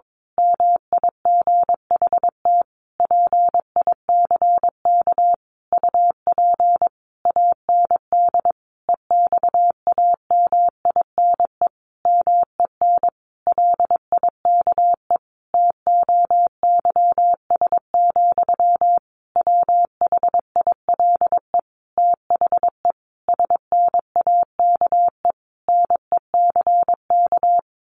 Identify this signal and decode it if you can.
Morse code, heard as a keyed tone, about 22 wpm, 700 Hz. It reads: MIGHT PICK UP AND EXAMINE MEN LIKE TOYS, WHILE THE SNAKE NECK
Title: A Sound Of Thunder 22wpm